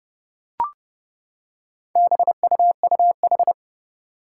Morse code, heard as a keyed tone, about 30 wpm, 690 Hz. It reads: BUUH
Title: Four Challenging Letter-Number Combinations 30wpm